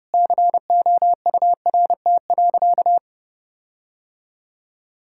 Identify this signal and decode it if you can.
Morse code, heard as a keyed tone, about 32 wpm, 700 Hz. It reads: COURT.
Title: Sentences from Top 1600 Words 30wpm